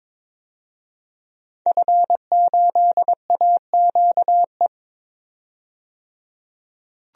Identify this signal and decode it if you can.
Morse code, heard as a keyed tone, about 22 wpm, 700 Hz. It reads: F8AQE
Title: DXCC-100 Call Signs - 5 Characters - 22wpm